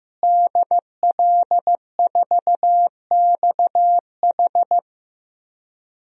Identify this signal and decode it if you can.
Morse code, heard as a keyed tone, about 15 wpm, 700 Hz. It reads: DL4XH
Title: DXCC-100 Call Signs - 5 Characters - Repeated 3x - 15wpm